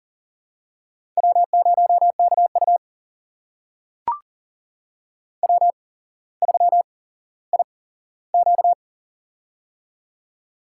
Morse code, heard as a keyed tone, about 38 wpm, spaced as for 17 wpm, 700 Hz. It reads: W0KU W3IQ
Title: USA Call Signs - 4 Characters - 8x Character Spacing - 40wpm